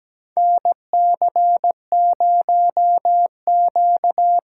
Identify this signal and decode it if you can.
Morse code, heard as a keyed tone, about 17 wpm, 700 Hz. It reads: NC0Q
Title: USA Call Signs - 4 Characters - 4x Character Spacing - 17wpm